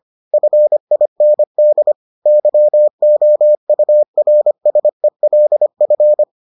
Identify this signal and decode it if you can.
Morse code, heard as a keyed tone, about 25 wpm, 600 Hz. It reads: FIND YOURSELF